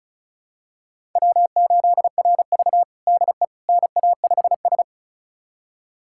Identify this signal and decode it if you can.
Morse code, heard as a keyed tone, about 36 wpm, 700 Hz. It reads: W8RV DE NA5S
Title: USA Callsign DE Callsign - 4 Characters - 35wpm